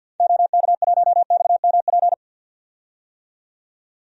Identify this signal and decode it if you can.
Morse code, heard as a keyed tone, about 47 wpm, 700 Hz. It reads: OK1XMP
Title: DXCC-100 Call Signs - 6 Characters - 50wpm